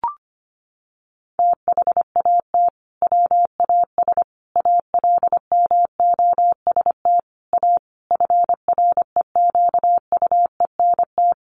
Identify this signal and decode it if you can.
Morse code, heard as a keyed tone, about 25 wpm, 700 Hz. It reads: THAT WAS ALMOST A FREQUENT
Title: Sentences from Top 1400 Words 25wpm